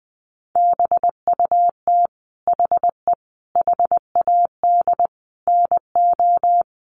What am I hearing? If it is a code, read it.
Morse code, heard as a keyed tone, about 20 wpm, 700 Hz. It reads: BUT HE HAD NO
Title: Sentences from Top 100 Words 20wpm